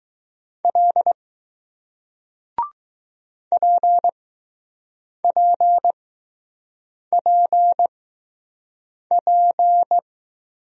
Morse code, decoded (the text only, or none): L P P P P